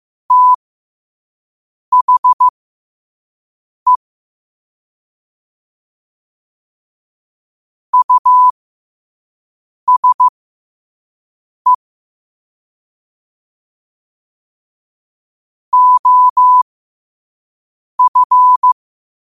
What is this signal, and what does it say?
Morse code, heard as a keyed tone, about 15 wpm, spaced as for 5 wpm, 1000 Hz. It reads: THE USE OF
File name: qotdfiles / 5wpm